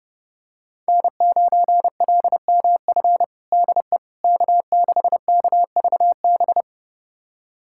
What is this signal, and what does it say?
Morse code, heard as a keyed tone, about 30 wpm, 700 Hz. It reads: N9LMF DE K6KVB
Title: USA Callsign DE Callsign - 5 Characters - 30wpm